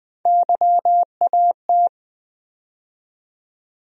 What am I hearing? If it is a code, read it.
Morse code, heard as a keyed tone, about 20 wpm, 700 Hz. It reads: YAT
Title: USA Call Sign Postfixes - 3 Letters 20wpm